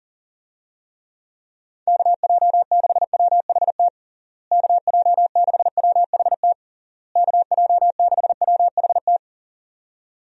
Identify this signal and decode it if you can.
Morse code, heard as a keyed tone, about 40 wpm, 700 Hz. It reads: KJ6WHT KJ6WHT KJ6WHT
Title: USA Call Signs - 6 Characters - Repeated 3x - 40wpm